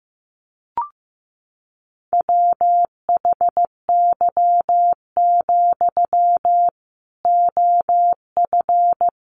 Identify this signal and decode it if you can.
Morse code, heard as a keyed tone, about 15 wpm, 700 Hz. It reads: WHY, OF
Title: Sentences from Top 500 Words 15wpm